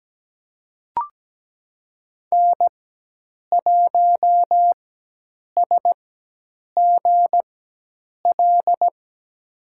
Morse code, heard as a keyed tone, about 17 wpm, spaced as for 8 wpm, 700 Hz. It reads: N1SGL
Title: USA Call Signs - 5 Characters - 4x Character Spacing - 17wpm